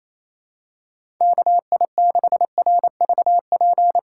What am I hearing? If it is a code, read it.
Morse code, heard as a keyed tone, about 28 wpm, 700 Hz. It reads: KI6RVP
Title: USA Call Signs - 6 Characters - 4x Character Spacing - 28wpm